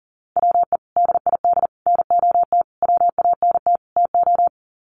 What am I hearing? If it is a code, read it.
Morse code, heard as a keyed tone, about 40 wpm, 700 Hz. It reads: WE DID NOT WANT TO
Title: Sets of 5 Words Encore 40wpm